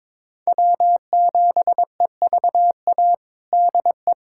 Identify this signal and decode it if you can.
Morse code, heard as a keyed tone, about 22 wpm, 700 Hz. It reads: W7EVA DE
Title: USA Callsign DE Callsign 22wpm